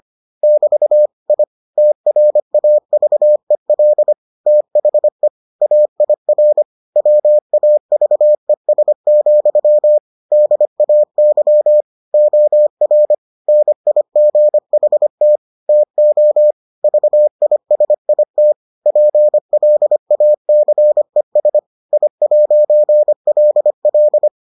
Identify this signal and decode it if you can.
Morse code, heard as a keyed tone, about 25 wpm, 600 Hz. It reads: <BT> I TRAVEL THE AIR WAVES, DAY OR NIGHT TO VISIT PLACES I'LL